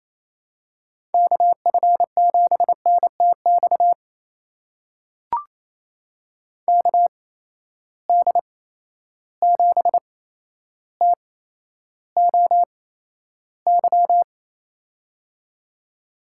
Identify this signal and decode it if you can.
Morse code, heard as a keyed tone, about 27 wpm, spaced as for 12 wpm, 700 Hz. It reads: KF7NTX KD7TOY
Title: USA Call Signs - 6 Characters - 8x Character Spacing - 28wpm